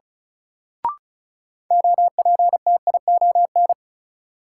Morse code, heard as a keyed tone, about 34 wpm, 700 Hz. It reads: OPTION
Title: Sentences from Top 1300 Words - New Word Drill Alt - 35wpm